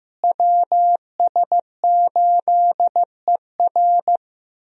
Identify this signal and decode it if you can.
Morse code, heard as a keyed tone, about 15 wpm, 700 Hz. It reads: WS8ER
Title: USA Call Signs - 5 Characters - Repeated 3x - 15wpm